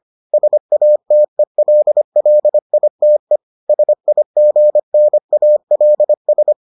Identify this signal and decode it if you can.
Morse code, heard as a keyed tone, about 25 wpm, 600 Hz. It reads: SATELLITE SIGNALS